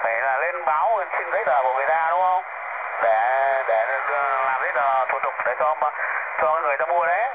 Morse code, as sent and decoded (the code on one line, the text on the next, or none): none